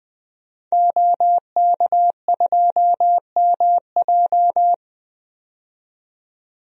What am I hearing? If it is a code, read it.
Morse code, heard as a keyed tone, about 20 wpm, 700 Hz. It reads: OK2MJ
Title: DXCC-100 Call Signs - 5 Characters - 20wpm